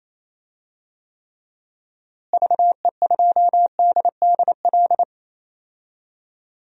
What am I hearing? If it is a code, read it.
Morse code, heard as a keyed tone, about 28 wpm, 700 Hz. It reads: VE2DDL
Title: DXCC-100 Call Signs - 6 Characters - Repeated 2x - 28wpm